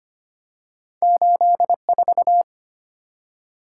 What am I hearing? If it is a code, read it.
Morse code, heard as a keyed tone, about 25 wpm, 700 Hz. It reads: 84
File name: Two Letter-Number Combinations - 25wpm